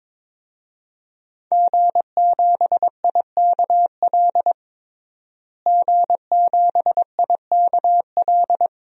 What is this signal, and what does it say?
Morse code, heard as a keyed tone, about 22 wpm, 700 Hz. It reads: G7IKL G7IKL
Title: DXCC-100 Call Signs - 5 Characters - Repeated 2x - 22wpm